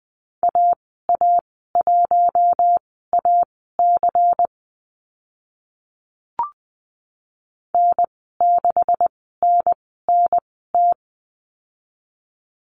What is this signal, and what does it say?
Morse code, heard as a keyed tone, about 20 wpm, spaced as for 14 wpm, 700 Hz. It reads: AA1AC N6NNT
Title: 154 Lesson 25g - Letter p - Call Signs 20wpm